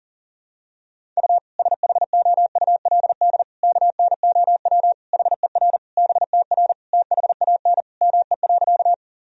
Morse code, heard as a keyed tone, about 40 wpm, 700 Hz. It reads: U SHOULD KNOW HER BTR THAN ME.
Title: Sentences from Top 400 Words and Abbreviations 40wpm